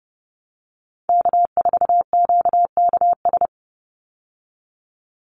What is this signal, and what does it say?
Morse code, heard as a keyed tone, about 30 wpm, 700 Hz. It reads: K4QKS
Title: USA Call Signs 30wpm